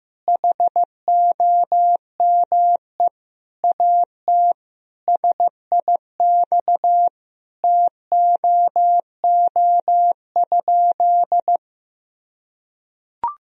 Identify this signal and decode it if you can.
Morse code, heard as a keyed tone, about 15 wpm, 700 Hz. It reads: HOME AT SIX TOO?
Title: Sentences from Top 1000 Words - Review Part 2 - 15wpm